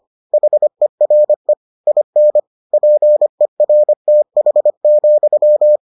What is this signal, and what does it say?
Morse code, heard as a keyed tone, about 25 wpm, 600 Hz. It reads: HERE IN PERTH,